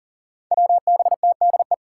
Morse code, heard as a keyed tone, about 40 wpm, 700 Hz. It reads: WBTDE
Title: Five Challenging Letter-Number Combinations 40wpm